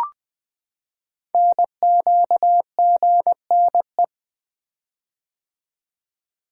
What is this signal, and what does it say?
Morse code, heard as a keyed tone, about 20 wpm, 700 Hz. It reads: NQGNE
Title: Code Groups - 5 Letters - 20wpm